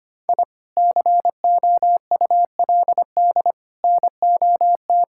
Morse code, heard as a keyed tone, about 25 wpm, 700 Hz. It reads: I COULD NOT
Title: Sentences from Top 1000 Words - Review Part 6 - 25wpm